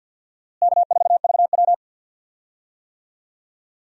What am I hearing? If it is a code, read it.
Morse code, heard as a keyed tone, about 50 wpm, 700 Hz. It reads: K4VW
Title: USA Call Signs - 4 Characters - Repeated 2x - 50wpm